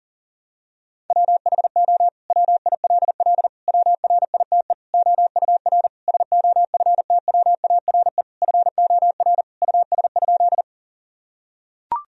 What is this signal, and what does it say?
Morse code, heard as a keyed tone, about 37 wpm, 700 Hz. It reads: WHO WILL WRITE OUR SOFTWARE FOR US?
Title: Sentences from Top 1000 Words - Review Part 3 - 40wpm